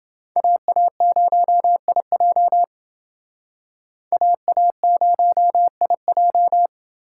Morse code, heard as a keyed tone, about 29 wpm, 700 Hz. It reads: AA0IJ AA0IJ